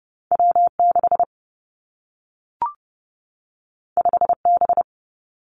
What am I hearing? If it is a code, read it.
Morse code, heard as a keyed tone, about 30 wpm, 700 Hz. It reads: W 6 5 B
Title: Two Challenging Letter-Number Combinations 30wpm